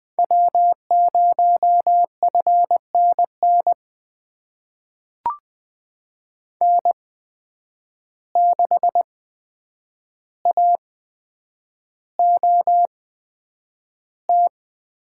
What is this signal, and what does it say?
Morse code, heard as a keyed tone, about 20 wpm, spaced as for 9 wpm, 700 Hz. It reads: W0FNN N6AOT
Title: USA Call Signs - 5 Characters - 8x Character Spacing - 20wpm